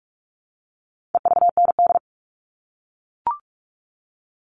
Morse code, heard as a keyed tone, about 44 wpm, 700 Hz. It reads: EVND
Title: All Cut Numbers - 4-Digit 45wpm